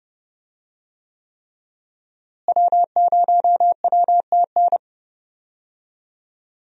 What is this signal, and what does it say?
Morse code, heard as a keyed tone, about 31 wpm, 700 Hz. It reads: W0WTN
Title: USA Call Signs - 5 Characters - Repeated 3x - 30wpm